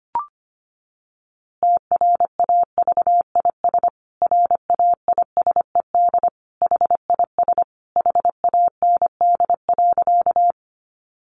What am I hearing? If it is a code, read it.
Morse code, heard as a keyed tone, about 25 wpm, 700 Hz. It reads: TRAVIS RAISED HIS HAND.
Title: A Sound Of Thunder 25wpm